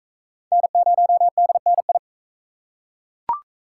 Morse code, heard as a keyed tone, about 41 wpm, 700 Hz. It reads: N0DNI